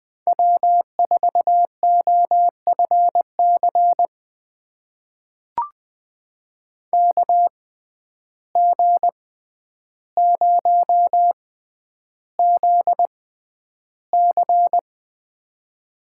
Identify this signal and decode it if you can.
Morse code, heard as a keyed tone, about 20 wpm, spaced as for 10 wpm, 700 Hz. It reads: W4OFC KG0ZC
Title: USA Call Signs - 5 Characters - 6x Character Spacing - 20wpm